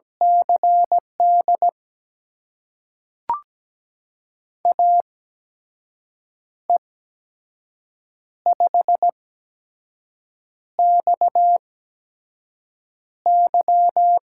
Morse code, heard as a keyed tone, about 17 wpm, spaced as for 5 wpm, 700 Hz. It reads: CD AE5XY
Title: USA Call Signs - 5 Characters - 8x Character Spacing - 17wpm